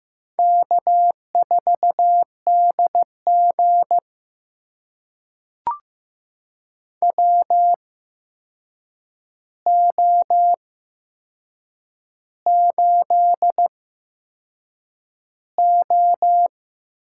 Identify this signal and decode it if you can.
Morse code, heard as a keyed tone, about 15 wpm, spaced as for 6 wpm, 700 Hz. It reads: K4DG WO8O
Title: USA Call Signs - 4 Characters - 8x Character Spacing - 15wpm